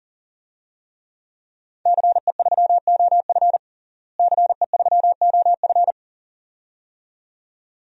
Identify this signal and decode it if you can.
Morse code, heard as a keyed tone, about 37 wpm, 700 Hz. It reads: CE3OF CE3OF
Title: DXCC-100 Call Signs - 5 Characters - Repeated 2x - 40wpm